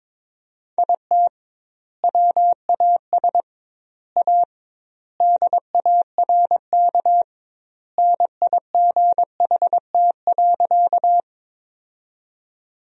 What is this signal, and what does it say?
Morse code, heard as a keyed tone, about 22 wpm, 700 Hz. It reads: IT WAS A DARK NIGHT.
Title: 2x Word-Spacing - Sentences from Top 500 Words - 22wpm